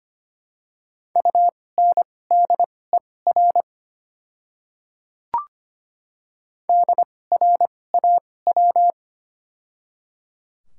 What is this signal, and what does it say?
Morse code, heard as a keyed tone, about 25 wpm, spaced as for 18 wpm, 700 Hz. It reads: UNDER DRAW
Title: 2x Character Spacing - Top 500 Words - 25wpm